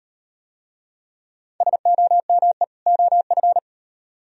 Morse code, heard as a keyed tone, about 34 wpm, 700 Hz. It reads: SOME OF